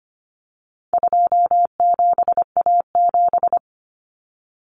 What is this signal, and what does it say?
Morse code, heard as a keyed tone, about 25 wpm, 700 Hz. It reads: 27A7
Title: Common Cut Numbers - 4-Digit 25wpm